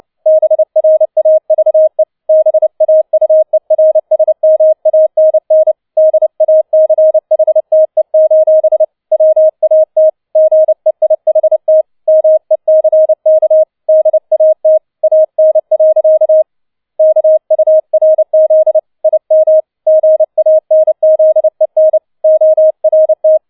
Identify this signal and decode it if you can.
Morse code, heard as a keyed tone, about 29 wpm, 630 Hz. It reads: BRAVE BAUERSMANN DACHTE: WAT GEIHT MECK DAT AN. KURZ IM GANZEN ORT
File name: 007 Letzter Streich 30wpm